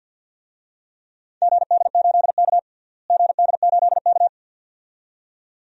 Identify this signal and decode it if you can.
Morse code, heard as a keyed tone, about 48 wpm, 700 Hz. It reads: GD8K GD8K
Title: DXCC-100 Call Signs - 4 Characters - Repeated 2x - 50wpm